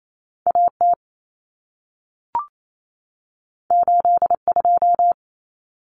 Morse code, heard as a keyed tone, about 28 wpm, 700 Hz. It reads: AT 82
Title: Common Cut Numbers - 2-Digit 28wpm